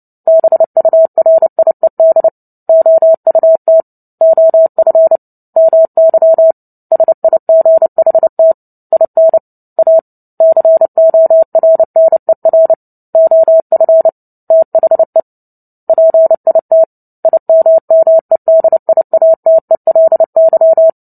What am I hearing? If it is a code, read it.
Morse code, heard as a keyed tone, about 30 wpm, 650 Hz. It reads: BURIED OUT OF MY SIGHT IN A CORNER OF THE PIT IMMEDIATELY
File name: Book Two / 03-Chapter 3 - 29 WPM